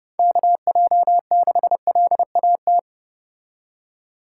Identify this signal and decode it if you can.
Morse code, heard as a keyed tone, about 31 wpm, 700 Hz. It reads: KJ6LAT